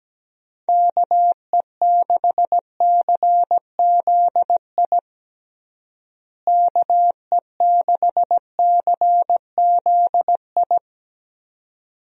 Morse code, heard as a keyed tone, about 17 wpm, 700 Hz. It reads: KE6CZI KE6CZI
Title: USA Call Signs - 6 Characters - Repeated 2x - 17wpm